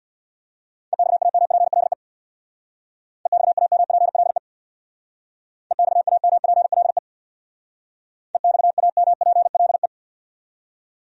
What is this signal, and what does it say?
Morse code, heard as a keyed tone, about 66 wpm, 700 Hz. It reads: EXAMPLE EXAMPLE EXAMPLE EXAMPLE